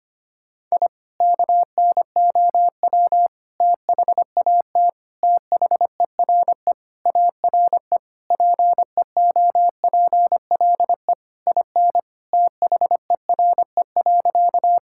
Morse code, heard as a keyed tone, about 25 wpm, 700 Hz. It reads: I KNOW THAT THERE ARE PEOPLE IN THERE.
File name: practice / 25wpm / Sentences with 3-Word Phrases - 25wpm